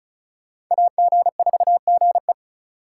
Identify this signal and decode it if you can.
Morse code, heard as a keyed tone, about 44 wpm, 700 Hz. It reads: AG4GE
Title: Speed Racing - USA 2x2 Call Signs 35wpm